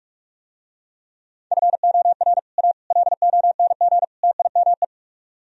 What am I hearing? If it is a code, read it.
Morse code, heard as a keyed tone, about 45 wpm, 700 Hz. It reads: FOR A LONG TIME